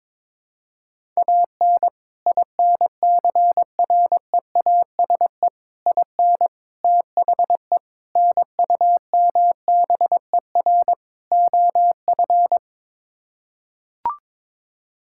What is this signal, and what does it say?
Morse code, heard as a keyed tone, about 22 wpm, 700 Hz. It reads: AN INCREASE IN THE NUMBER OF
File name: practice / 22wpm / Sets of 6 Words - 22wpm